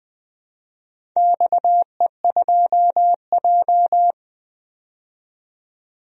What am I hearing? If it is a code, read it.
Morse code, heard as a keyed tone, about 20 wpm, 700 Hz. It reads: XE2J
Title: DXCC-100 Call Signs - 4 Characters - 20wpm